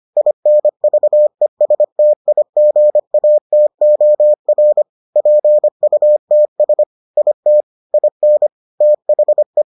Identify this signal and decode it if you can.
Morse code, heard as a keyed tone, about 25 wpm, 600 Hz. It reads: INVESTIGATOR PUTS IT IN THE